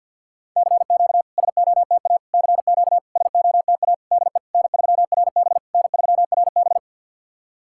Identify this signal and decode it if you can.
Morse code, heard as a keyed tone, about 50 wpm, 690 Hz. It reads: CQ SOTA CQ SOTA DE N3RB N3RB
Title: SOTA QSO Practice 50wpm